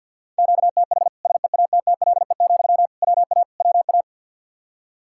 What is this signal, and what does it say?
Morse code, heard as a keyed tone, about 48 wpm, 700 Hz. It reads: QTH SEATTLE, WA WA
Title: Practice QSO - Exchange 1 - Beginning - 50wpm